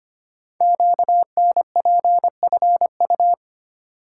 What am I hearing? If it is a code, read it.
Morse code, heard as a keyed tone, about 25 wpm, 700 Hz. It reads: QNPFU
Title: Code Groups - 5 Letters - 25wpm